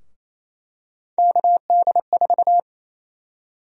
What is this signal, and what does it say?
Morse code, heard as a keyed tone, about 27 wpm, 700 Hz. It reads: KD4
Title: USA Call Sign Prefixes - Repeated 2x - 28wpm